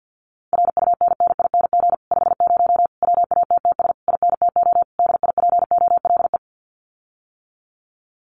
Morse code, heard as a keyed tone, about 48 wpm, 700 Hz. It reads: RUNNING 50 WATTS INTO DIPOLE
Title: Common QSO Phrases Encore 50wpm